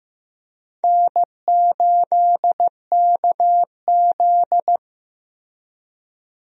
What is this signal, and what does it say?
Morse code, heard as a keyed tone, about 15 wpm, 700 Hz. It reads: N8KZ
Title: USA Call Signs - 4 Characters - Repeated 2x - 15wpm